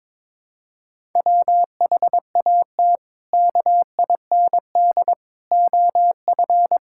Morse code, heard as a keyed tone, about 22 wpm, 700 Hz. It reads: WHAT KIND OF
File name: practice / 22wpm / Sentences with 3-Word Phrases - 22wpm